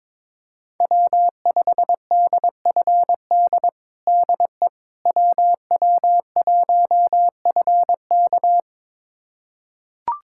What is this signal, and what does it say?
Morse code, heard as a keyed tone, about 22 wpm, 700 Hz. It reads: W5DFD DE WW1FK
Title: USA Callsign DE Callsign - 5 Characters - 22wpm